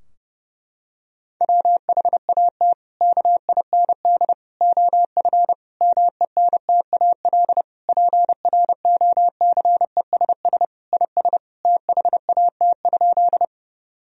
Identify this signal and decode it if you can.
Morse code, heard as a keyed tone, about 30 wpm, 700 Hz. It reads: WHAT KIND OF MENTAL PROCESS IS THAT?
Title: Sentences from Top 1000 Words - Review Part 4 - 30wpm